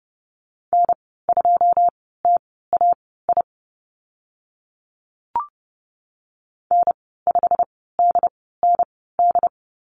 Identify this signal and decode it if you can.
Morse code, heard as a keyed tone, about 30 wpm, spaced as for 17 wpm, 700 Hz. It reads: N2TAI N5DND
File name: Lesson 15g - Number 5 - Call Signs 30wpm